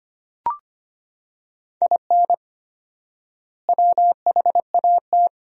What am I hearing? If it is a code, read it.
Morse code, heard as a keyed tone, about 25 wpm, 700 Hz. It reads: IN WHAT
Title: 4x Word-Spacing - Sentences from Top 200 Words - 25wpm